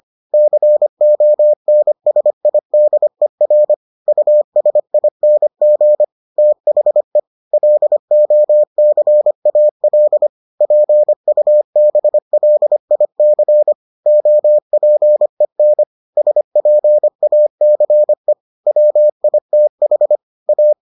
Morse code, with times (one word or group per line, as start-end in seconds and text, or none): 0.3-3.7 CONSIDER
4.1-6.0 USING
6.4-7.2 THE
7.5-10.3 LOCAL
10.6-13.7 PUBLIC
14.1-15.8 OPEN
16.2-18.3 SPACE
18.7-20.2 WITH
20.5-20.7 A